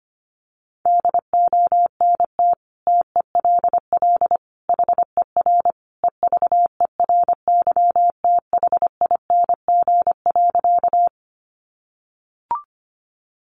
Morse code, heard as a keyed tone, about 25 wpm, 700 Hz. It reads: DONT TELL HER EVERYTHING.
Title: Sentences from Top 700 Words 25wpm